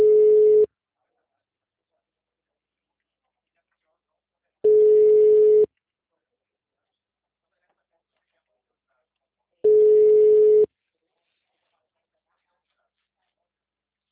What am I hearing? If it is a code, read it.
Ringback tone (Europe, 425 Hz), with speech.